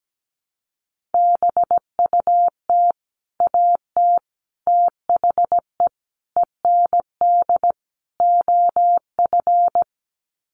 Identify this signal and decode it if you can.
Morse code, heard as a keyed tone, about 17 wpm, 700 Hz. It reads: BUT AT THE END OF
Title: Sets of 5 Words Encore2 17wpm